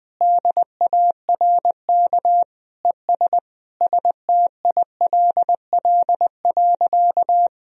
Morse code, heard as a keyed tone, about 21 wpm, 700 Hz. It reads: DARK ES STILL.
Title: Sentences from Top 500 Words and Abbreviations 20wpm